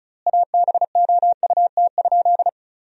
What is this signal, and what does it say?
Morse code, heard as a keyed tone, about 36 wpm, 700 Hz. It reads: ABOUT?
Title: 4x Word-Spacing - Sentences from Top 100 Words - 35wpm